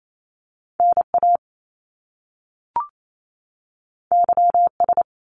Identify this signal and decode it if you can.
Morse code, heard as a keyed tone, about 28 wpm, 700 Hz. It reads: NA YS
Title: Top Two Letter Combinations in Words 28wpm